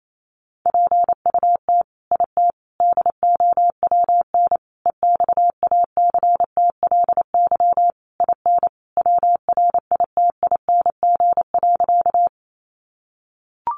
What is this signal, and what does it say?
Morse code, heard as a keyed tone, about 28 wpm, 700 Hz. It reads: PUT IT DOWN EXACTLY IN WRITING.
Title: Sentences from Top 900 Words 28wpm